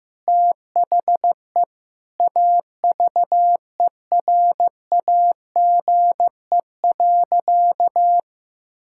Morse code, heard as a keyed tone, about 15 wpm, 700 Hz. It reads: THE AVERAGE.
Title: Sentences from Top 1000 Words - Review Part 6 - 15wpm